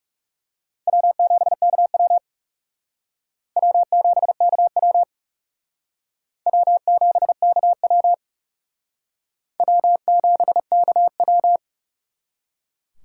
Morse code, heard as a keyed tone, about 39 wpm, 700 Hz. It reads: W7KW W7KW W7KW W7KW